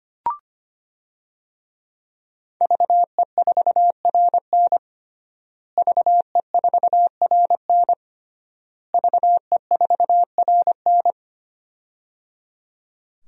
Morse code, heard as a keyed tone, about 25 wpm, 700 Hz. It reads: VE4RN VE4RN VE4RN
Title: DXCC-100 Call Signs - 5 Characters - Repeated 3x - 25wpm